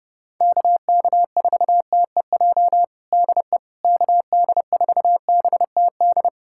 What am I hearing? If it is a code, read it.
Morse code, heard as a keyed tone, about 32 wpm, 700 Hz. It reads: KK4TEJ DE KD4BTD
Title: USA Callsign DE Callsign - 6 Characters - 30wpm